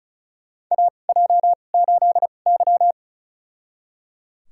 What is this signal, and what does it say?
Morse code, heard as a keyed tone, about 34 wpm, spaced as for 25 wpm, 700 Hz. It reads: AJ8Y
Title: USA Call Signs - 2x Character Spacing - 35wpm